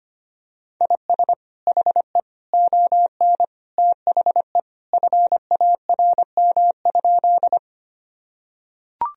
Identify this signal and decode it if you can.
Morse code, heard as a keyed tone, about 25 wpm, 700 Hz. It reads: IS HE ON THE FARM?
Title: Sentences from Top 1000 Words - Review Part 2 - 25wpm